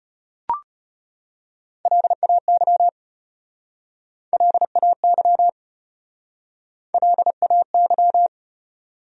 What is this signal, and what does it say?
Morse code, heard as a keyed tone, about 33 wpm, 700 Hz. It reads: LAY LAY LAY